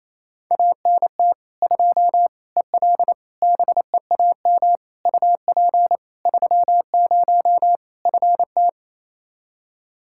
Morse code, heard as a keyed tone, about 27 wpm, 700 Hz. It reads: ANT 2 EL BEAM UP 30 FT
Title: Practice QSO - Exchange 2 - Info - 28wpm